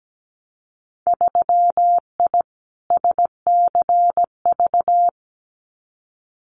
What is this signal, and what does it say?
Morse code, heard as a keyed tone, about 17 wpm, 700 Hz. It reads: 3I SCV
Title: Winter Field Day Exchanges Encore 17wpm